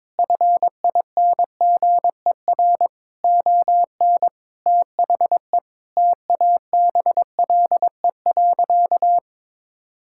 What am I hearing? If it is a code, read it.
Morse code, heard as a keyed tone, about 22 wpm, 700 Hz. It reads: FINGER ON THE TABLE.
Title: Sentences from Top 1600 Words 22wpm